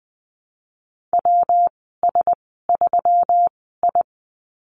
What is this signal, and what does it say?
Morse code, heard as a keyed tone, about 19 wpm, spaced as for 14 wpm, 700 Hz. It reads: WS3I
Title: 161 Lesson 26g - Letter g - Call Signs 20wpm